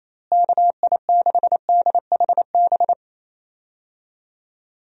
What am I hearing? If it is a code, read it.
Morse code, heard as a keyed tone, about 28 wpm, 700 Hz. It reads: KI6DHB
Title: USA Call Signs - 6 Characters - Repeated 3x - 28wpm